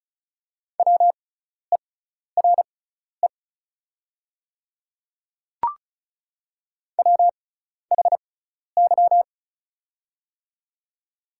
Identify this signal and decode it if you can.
Morse code, heard as a keyed tone, about 34 wpm, spaced as for 12 wpm, 700 Hz. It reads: WERE WHY
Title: 6x Character Spacing - Top 200 Words - 35wpm